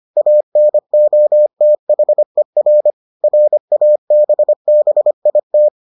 Morse code, heard as a keyed tone, about 25 wpm, 600 Hz. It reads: ANOTHER RABBIT